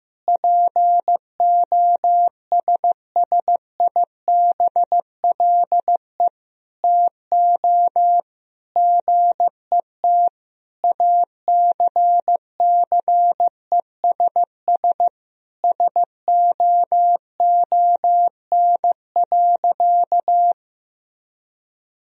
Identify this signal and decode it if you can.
Morse code, heard as a keyed tone, about 15 wpm, 700 Hz. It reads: POSSIBLE TO GET ACCESS SOON.
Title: Sentences from Top 1000 Words - Review Part 4 - 15wpm